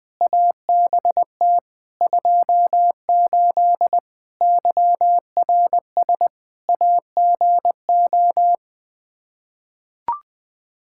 Morse code, heard as a keyed tone, about 20 wpm, 700 Hz. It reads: ABT 28 YRS AGO
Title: Practice QSO - Exchange 3 - More Info - 20wpm